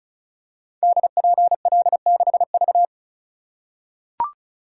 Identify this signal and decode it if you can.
Morse code, heard as a keyed tone, about 36 wpm, 700 Hz. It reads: DPL6V
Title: Code Groups - 5 Letter:Numbers - 35wpm